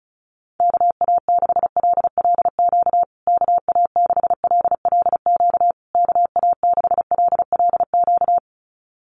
Morse code, heard as a keyed tone, about 34 wpm, 700 Hz. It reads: KA6LLQ KA6LLQ KA6LLQ
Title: USA Call Signs Repeated 3x 35wpm